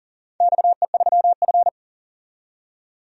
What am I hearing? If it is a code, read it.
Morse code, heard as a keyed tone, about 40 wpm, 700 Hz. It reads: XE3F
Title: DXCC-100 Call Signs - 4 Characters - 40wpm